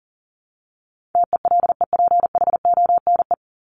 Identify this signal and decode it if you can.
Morse code, heard as a keyed tone, about 40 wpm, 700 Hz. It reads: TELEPHONE
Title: Top 1501-1750 Words 40wpm